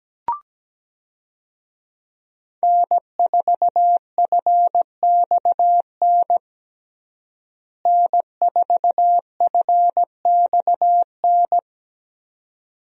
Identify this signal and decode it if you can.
Morse code, heard as a keyed tone, about 17 wpm, 700 Hz. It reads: N4FXN N4FXN
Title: USA Call Signs - 5 Characters - Repeated 2x - 17wpm